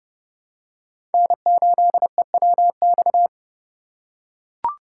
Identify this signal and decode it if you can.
Morse code, heard as a keyed tone, about 31 wpm, 700 Hz. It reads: N8EWX
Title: K1USN SST QSO Practice - 30wpm